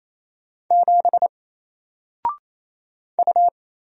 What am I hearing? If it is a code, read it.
Morse code, heard as a keyed tone, about 28 wpm, 700 Hz. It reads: 7 U
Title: Single Character (Letter:Number:Punctuation:Prosign) Alt 28wpm